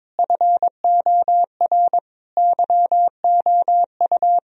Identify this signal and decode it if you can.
Morse code, heard as a keyed tone, about 22 wpm, 700 Hz. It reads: FOR YOU
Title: Sentences from Top 1000 Words - Review Part 3 - 22wpm